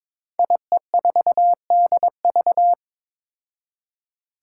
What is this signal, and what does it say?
Morse code, heard as a keyed tone, about 23 wpm, 700 Hz. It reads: IE4DV
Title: Five Challenging Letter-Number Combinations 22wpm